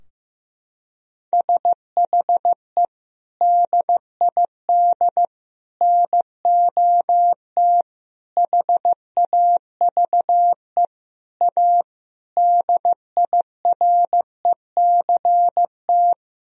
Morse code, heard as a keyed tone, about 15 wpm, 700 Hz. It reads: SHE DID NOT HAVE A DIRECT
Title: Sentences from Top 1000 Words - Review Part 2 - 15wpm